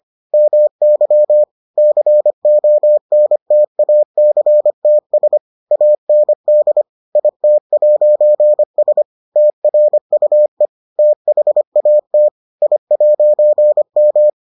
Morse code, heard as a keyed tone, about 25 wpm, 600 Hz. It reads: MY CONTACTS AND IT'S TRUE THAT I'M